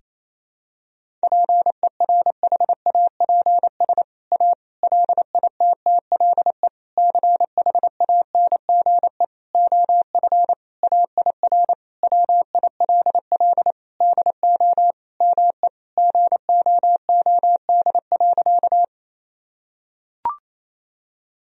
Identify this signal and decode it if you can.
Morse code, heard as a keyed tone, about 28 wpm, 700 Hz. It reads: PERHAPS A LITTLE CHANGE OF AIR WILL DO ME GOOD.
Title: Sentences from Top 1000 Words - Review Part 1 - 28wpm